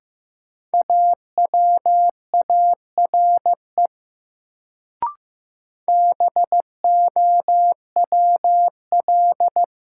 Morse code, heard as a keyed tone, about 15 wpm, 700 Hz. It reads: AWARE BOWL
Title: Sentences from Top 1200 Words - New Word Drill - 15wpm